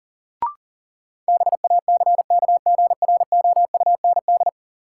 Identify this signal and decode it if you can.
Morse code, heard as a keyed tone, about 40 wpm, 700 Hz. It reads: BACKGROUND
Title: Sentences from Top 1300 Words - New Word Drill Alt - 40wpm